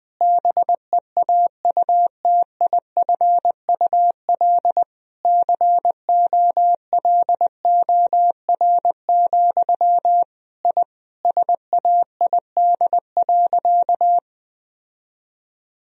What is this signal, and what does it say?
Morse code, heard as a keyed tone, about 20 wpm, 700 Hz. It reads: BEAUTIFUL COLOR, I SAID.
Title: Sentences from Top 1000 Words - Review Part 2 - 20wpm